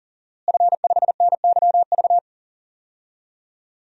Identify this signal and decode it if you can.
Morse code, heard as a keyed tone, about 40 wpm, 700 Hz. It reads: F5NYV
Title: DXCC-100 Call Signs - 5 Characters - Repeated 3x - 40wpm